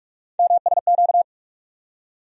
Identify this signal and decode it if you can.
Morse code, heard as a keyed tone, about 42 wpm, 700 Hz. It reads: MSQ
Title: USA Call Sign Postfixes - 3 Letters 45wpm